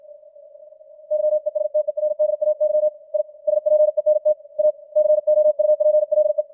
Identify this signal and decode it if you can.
Morse code, heard as a keyed tone, about 56 wpm, 610 Hz. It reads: YESTERDAY I SPENT A COUPLE
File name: archived / CWBros